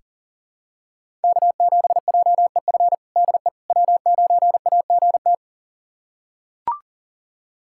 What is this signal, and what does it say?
Morse code, heard as a keyed tone, about 40 wpm, 700 Hz. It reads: K7JEF DE W9AGT
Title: USA Callsign DE Callsign - 5 Characters - 40wpm